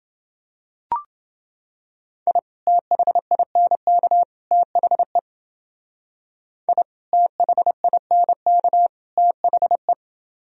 Morse code, heard as a keyed tone, about 29 wpm, 700 Hz. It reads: I THINK THE I THINK THE